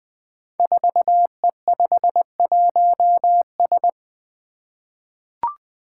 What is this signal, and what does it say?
Morse code, heard as a keyed tone, about 20 wpm, 700 Hz. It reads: 4E51S
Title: Five Challenging Letter-Number Combinations 20wpm